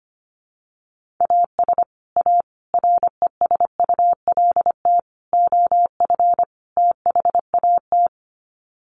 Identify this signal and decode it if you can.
Morse code, heard as a keyed tone, about 25 wpm, 700 Hz. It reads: AS A RESULT OF THAT
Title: Sets of 5 Words Encore 25wpm